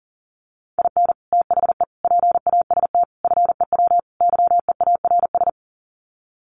Morse code, heard as a keyed tone, about 40 wpm, 700 Hz. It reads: IN THE PAST FEW YEARS
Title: Sets of 5 Words 40wpm